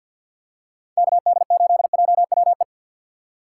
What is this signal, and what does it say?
Morse code, heard as a keyed tone, about 50 wpm, 700 Hz. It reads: KD8JWE
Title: USA Call Signs - 2x Character Spacing - 50wpm